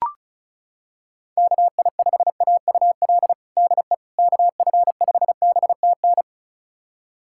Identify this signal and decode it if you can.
Morse code, heard as a keyed tone, about 36 wpm, 700 Hz. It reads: KI5AUL DE KF5BTN
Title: USA Callsign DE Callsign - 6 Characters - 35wpm